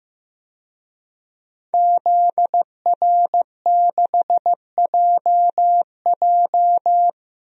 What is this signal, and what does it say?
Morse code, heard as a keyed tone, about 15 wpm, 700 Hz. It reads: ZR6JJ
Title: DXCC-100 Call Signs - 5 Characters - 15wpm